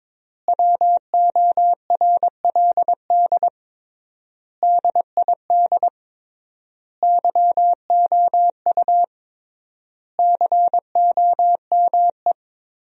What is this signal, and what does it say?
Morse code, heard as a keyed tone, about 22 wpm, 700 Hz. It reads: WORLD DID YOU COME
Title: 3x Word-Spacing - Sentences from Top 200 Words - 22wpm